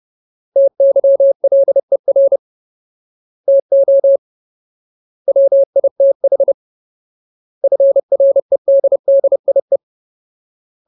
Morse code, heard as a keyed tone, about 30 wpm, 550 Hz. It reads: TYLER TO WITH FREDDIE
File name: Ultimate Word List / 30WPM Ultimate